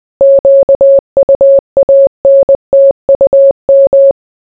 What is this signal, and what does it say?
Morse code, heard as a keyed tone, about 20 wpm, 550 Hz. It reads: QUANTUM